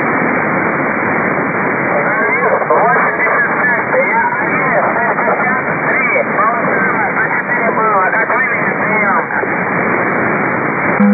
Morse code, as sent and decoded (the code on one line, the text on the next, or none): none